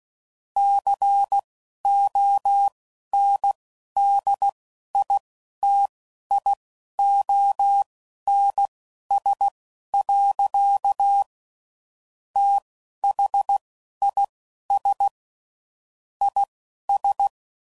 Morse code, heard as a keyed tone, about 16 wpm, spaced as for 11 wpm, 800 Hz. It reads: CONDITIONS. THIS IS
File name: CW Code Groups @10WPM/Technical Subjects